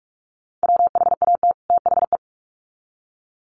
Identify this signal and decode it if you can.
Morse code, heard as a keyed tone, about 44 wpm, 700 Hz. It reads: WHAT THE
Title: Sets of 2 Words 45wpm